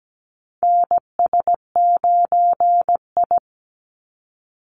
Morse code, heard as a keyed tone, about 17 wpm, 700 Hz. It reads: NS9I
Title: Call Signs - 17wpm